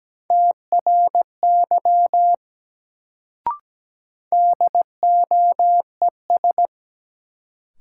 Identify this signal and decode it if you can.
Morse code, heard as a keyed tone, about 17 wpm, 700 Hz. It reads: TRY DOES
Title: Sentences from Top 200 Words - New Word Drill - 17wpm